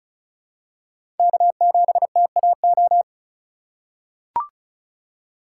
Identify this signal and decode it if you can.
Morse code, heard as a keyed tone, about 34 wpm, 700 Hz. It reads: K7TAO
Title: USA Call Signs - 5 Characters - 4x Character Spacing - 35wpm